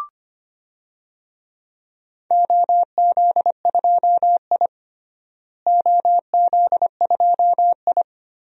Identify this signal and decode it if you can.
Morse code, heard as a keyed tone, about 25 wpm, 700 Hz. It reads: OZ2I OZ2I
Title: DXCC-100 Call Signs - 4 Characters - Repeated 3x - 25wpm